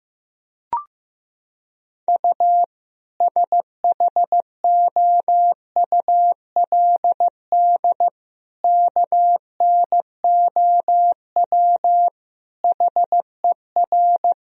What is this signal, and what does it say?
Morse code, heard as a keyed tone, about 15 wpm, 700 Hz. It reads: U SHOULD KNOW HER
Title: Sentences from Top 400 Words and Abbreviations 15wpm